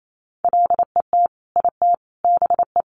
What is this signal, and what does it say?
Morse code, heard as a keyed tone, about 28 wpm, 700 Hz. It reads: LET IT BE
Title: Sentences from Top 1400 Words 28wpm